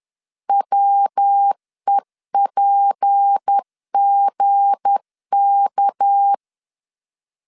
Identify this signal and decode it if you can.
Morse code, heard as a keyed tone, about 10 wpm, 800 Hz. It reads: WEPGK
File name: LEZIONI / 8 YJ45 / 8 YJ4 10wpm 2633